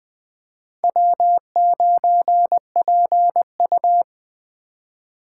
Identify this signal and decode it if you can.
Morse code, heard as a keyed tone, about 21 wpm, 700 Hz. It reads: W9PU
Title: Speed Racing - USA 1x2 Call Signs 17wpm